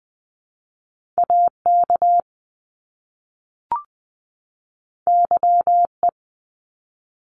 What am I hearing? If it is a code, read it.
Morse code, heard as a keyed tone, about 20 wpm, 700 Hz. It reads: AK YE
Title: Top Two Letter Combinations in Words 20wpm